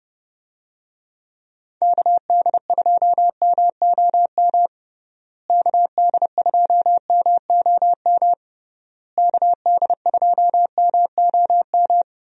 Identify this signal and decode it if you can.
Morse code, heard as a keyed tone, about 30 wpm, 700 Hz. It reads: KD2MOM KD2MOM KD2MOM
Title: USA Call Signs - 6 Characters - Repeated 3x - 30wpm